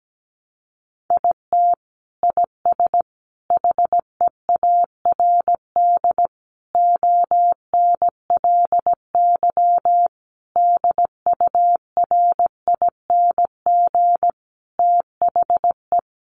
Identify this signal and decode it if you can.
Morse code, heard as a keyed tone, about 17 wpm, 700 Hz. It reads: IT IS HEARD ONLY DURING THE
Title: Sentences from Top 1200 Words 17wpm